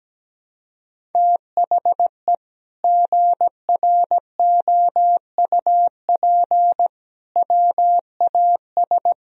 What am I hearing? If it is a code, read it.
Morse code, heard as a keyed tone, about 17 wpm, 700 Hz. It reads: THE GROUP WAS
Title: Sentences from Top 1000 Words - Review Part 1 - 17wpm